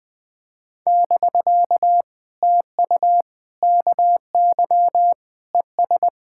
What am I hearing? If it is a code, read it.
Morse code, heard as a keyed tone, about 21 wpm, 700 Hz. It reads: <BK> TU KY ES
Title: POTA QSO Practice - 20wpm